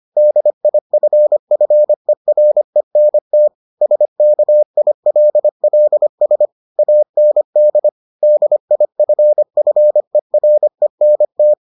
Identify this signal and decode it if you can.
Morse code, heard as a keyed tone, about 25 wpm, 600 Hz. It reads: DIFFERENT SKILLS AND DIFFERENT